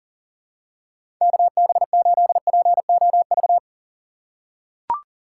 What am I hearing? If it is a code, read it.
Morse code, heard as a keyed tone, about 40 wpm, 700 Hz. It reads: KB8POV
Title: USA Call Signs - 6 Characters - 8x Character Spacing - 40wpm